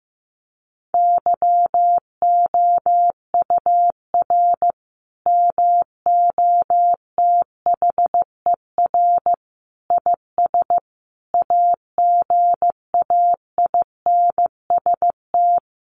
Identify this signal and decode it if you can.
Morse code, heard as a keyed tone, about 15 wpm, 700 Hz. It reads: YOUR MOTHER IS AGAINST
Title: Sentences from Top 1400 Words 15wpm